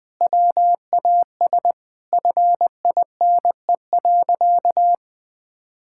Morse code, heard as a keyed tone, about 20 wpm, 700 Hz. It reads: WAS FINE.
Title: Sentences from Top 1000 Words - Review Part 5 - 20wpm